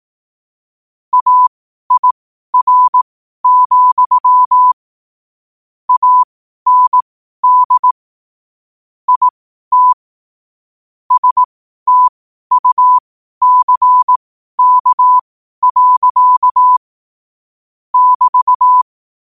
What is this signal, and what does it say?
Morse code, heard as a keyed tone, about 18 wpm, spaced as for 12 wpm, 1000 Hz. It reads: AIR, AND IT STUCK. <BT>
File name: qotdfiles / 12wpm